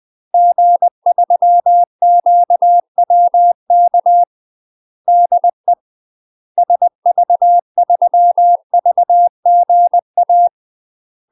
Morse code, heard as a keyed tone, about 20 wpm, 700 Hz. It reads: G3QWK DE SV3VGA
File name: UK MASTER QSOs TRAINING / 20WPM / UK MASTER QSOs 0059